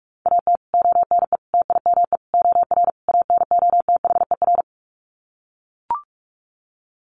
Morse code, heard as a keyed tone, about 44 wpm, 700 Hz. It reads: AT ONE TIME OR ANOTHER
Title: Sets of 5 Words Encore 45wpm